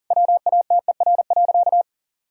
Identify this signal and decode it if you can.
Morse code, heard as a keyed tone, about 40 wpm, 700 Hz. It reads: WATER.
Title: Sentences from Top 1000 Words - Review Part 1 - 40wpm